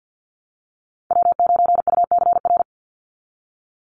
Morse code, heard as a keyed tone, about 49 wpm, 700 Hz. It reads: W9UCR
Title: Call Signs Repeated 3x - 50wpm